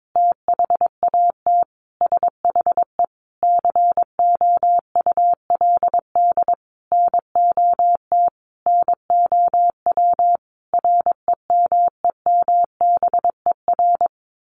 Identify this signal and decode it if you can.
Morse code, heard as a keyed tone, about 22 wpm, 700 Hz. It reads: THAT SHE COULD NOT NOW REMEMBER
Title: Sentences from Top 1200 Words 22wpm